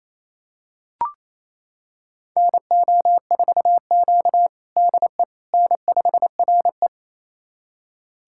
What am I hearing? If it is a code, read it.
Morse code, heard as a keyed tone, about 28 wpm, 700 Hz. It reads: NO4Q DE N5RE
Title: USA Callsign DE Callsign - 4 Characters - 28wpm